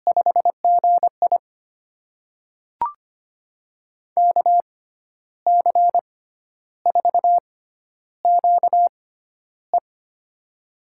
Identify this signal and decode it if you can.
Morse code, heard as a keyed tone, about 25 wpm, spaced as for 11 wpm, 700 Hz. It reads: 5GI KC4QE
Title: USA Call Signs - 6x Character Spacing - 25wpm